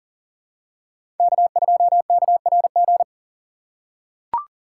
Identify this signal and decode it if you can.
Morse code, heard as a keyed tone, about 40 wpm, 700 Hz. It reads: K2KRG
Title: USA Call Signs - 2x Character Spacing - 40wpm